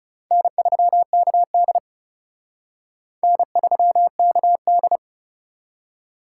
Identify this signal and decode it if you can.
Morse code, heard as a keyed tone, about 36 wpm, 700 Hz. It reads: N3KD N3KD
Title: Speed Racing - USA 1x2 Call Signs 30wpm